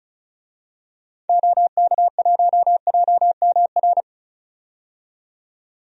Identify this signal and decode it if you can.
Morse code, heard as a keyed tone, about 34 wpm, 700 Hz. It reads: OK1JMR
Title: DXCC-100 Call Signs - 6 Characters - 35wpm